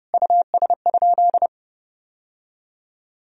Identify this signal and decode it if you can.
Morse code, heard as a keyed tone, about 32 wpm, 700 Hz. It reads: US?
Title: Sentences from Top 1000 Words - Review Part 2 - 30wpm